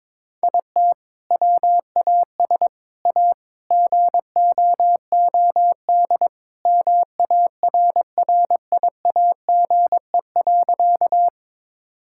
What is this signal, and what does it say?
Morse code, heard as a keyed tone, about 22 wpm, 700 Hz. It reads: IT WAS A GOOD MARRIAGE.
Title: Sentences from Top 1000 Words - Review Part 5 - 22wpm